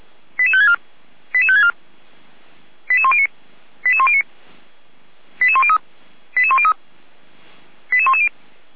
Five-tone selective call (ZVEI), heard as quick runs of five tones, tones of about 70 ms. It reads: 88564 88564 88109 88109 88103 88103 88110